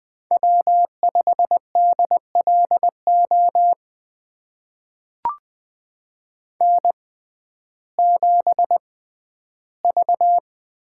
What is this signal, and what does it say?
Morse code, heard as a keyed tone, about 20 wpm, spaced as for 12 wpm, 700 Hz. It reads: W5DLO N7V
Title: USA Call Signs - 5 Characters - 6x Character Spacing - 20wpm